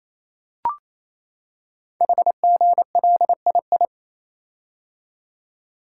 Morse code, heard as a keyed tone, about 27 wpm, 700 Hz. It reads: HGLII
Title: Code Groups - 5 Letters - 28wpm